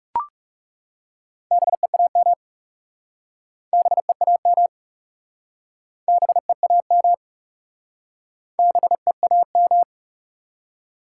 Morse code, heard as a keyed tone, about 40 wpm, 700 Hz. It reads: BEAM BEAM BEAM BEAM